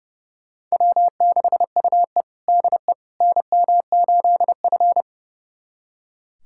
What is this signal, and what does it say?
Morse code, heard as a keyed tone, about 32 wpm, 700 Hz. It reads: W6UE DE NM8F
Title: USA Callsign DE Callsign - 4 Characters - 30wpm